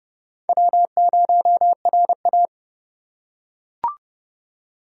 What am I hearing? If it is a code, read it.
Morse code, heard as a keyed tone, about 27 wpm, 700 Hz. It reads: W0RA